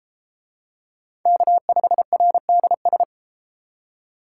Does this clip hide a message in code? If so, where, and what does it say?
Yes: Morse code from 1.3 to 3.0 s, heard as a keyed tone, about 33 wpm, 700 Hz. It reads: K5RDS